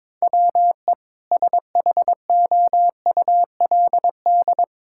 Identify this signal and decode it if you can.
Morse code, heard as a keyed tone, about 22 wpm, 700 Hz. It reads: WE SHOULD
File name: practice / 22wpm / Sentences with 2-Word Phrases - 22wpm